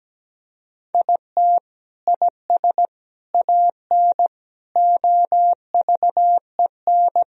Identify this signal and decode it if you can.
Morse code, heard as a keyed tone, about 17 wpm, 700 Hz. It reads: IT IS AN OVEN
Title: Sentences from Top 1000 Words - Review Part 4 - 17wpm